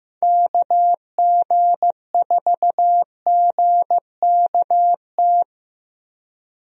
Morse code, heard as a keyed tone, about 16 wpm, 700 Hz. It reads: KG4GKT
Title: USA Call Signs - 6 Characters - Repeated 2x - 15wpm